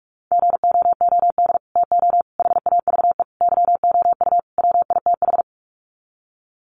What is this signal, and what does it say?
Morse code, heard as a keyed tone, about 44 wpm, 700 Hz. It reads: GOOD TO HAVE YOU WITH
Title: Sets of 5 Words Encore2 45wpm